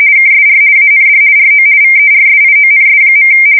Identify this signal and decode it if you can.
RTTY, about 50 baud. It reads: KSR AND RO VERSIONS.
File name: fsk170-50B / teletype-corp